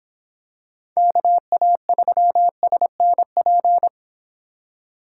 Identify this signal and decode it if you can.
Morse code, heard as a keyed tone, about 26 wpm, 700 Hz. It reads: KA3SNP